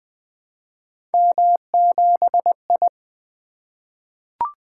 Morse code, heard as a keyed tone, about 20 wpm, 700 Hz. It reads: M7I
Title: Three Letter-Number Combinations - 20wpm